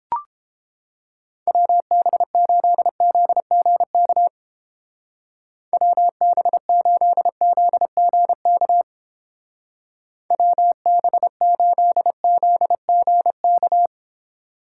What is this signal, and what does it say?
Morse code, heard as a keyed tone, about 27 wpm, 700 Hz. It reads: WB8ZGK WB8ZGK WB8ZGK
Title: Speed Racing - USA 2x3 Call Signs 22wpm